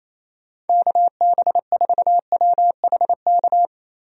Morse code, heard as a keyed tone, about 27 wpm, 700 Hz. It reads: KB4WHK
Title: USA Call Signs - 6 Characters - 28wpm